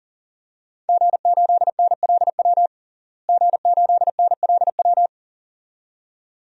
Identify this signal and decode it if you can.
Morse code, heard as a keyed tone, about 37 wpm, 700 Hz. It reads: G8NLW G8NLW
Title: DXCC-100 Call Signs - 5 Characters - Repeated 2x - 40wpm